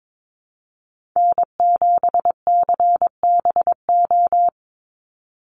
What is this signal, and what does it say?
Morse code, heard as a keyed tone, about 22 wpm, 700 Hz. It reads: N7CBO
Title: USA Call Signs 22wpm